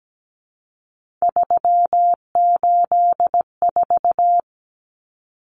Morse code, heard as a keyed tone, about 17 wpm, 700 Hz. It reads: 384
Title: Common Cut Numbers - 3-Digit 17wpm